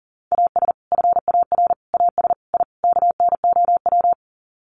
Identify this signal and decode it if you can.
Morse code, heard as a keyed tone, about 40 wpm, 700 Hz. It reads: AS FAR AS I KNOW
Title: Sets of 5 Words 40wpm